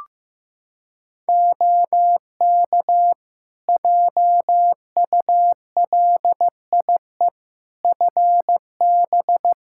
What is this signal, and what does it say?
Morse code, heard as a keyed tone, about 15 wpm, 700 Hz. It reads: OK JULIE FB
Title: Practice QSO - Exchange 2 - Info - 15wpm